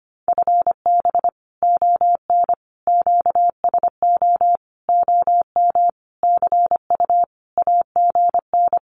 Morse code, heard as a keyed tone, about 25 wpm, 700 Hz. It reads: FB ON QSO OM CU AGN
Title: Common QSO Phrases 25wpm